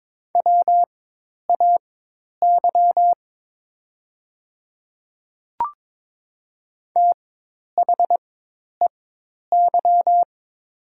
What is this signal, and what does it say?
Morse code, heard as a keyed tone, about 22 wpm, spaced as for 10 wpm, 700 Hz. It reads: WAY THEY
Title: 4x Character Spacing - Top 100 Words - 22wpm